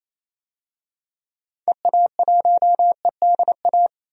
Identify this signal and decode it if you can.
Morse code, heard as a keyed tone, about 28 wpm, 700 Hz. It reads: EA1EDA
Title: DXCC-100 Call Signs - 6 Characters - 28wpm